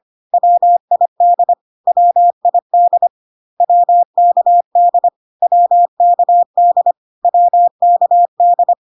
Morse code, heard as a keyed tone, about 25 wpm, 700 Hz. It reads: WID WID WKD WKD WKD
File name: cw / CW common abbreviations alphabetical repeated 3 times 25 25wpm ik1hge